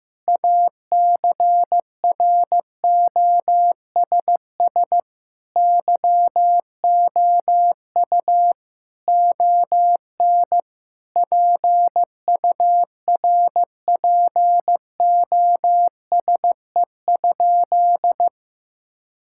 Morse code, heard as a keyed tone, about 15 wpm, 700 Hz. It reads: ACROSS YOU ON PURPOSE?
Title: Sentences from Top 1000 Words - Review Part 5 - 15wpm